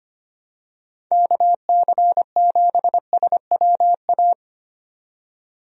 Morse code, heard as a keyed tone, about 25 wpm, 700 Hz. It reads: KC7SWA
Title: USA Call Signs - 6 Characters - 25wpm